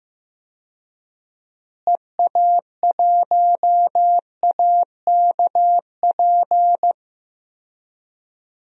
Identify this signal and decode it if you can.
Morse code, heard as a keyed tone, about 15 wpm, 700 Hz. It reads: EA1AKP
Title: DXCC-100 Call Signs - 6 Characters - Repeated 3x - 15wpm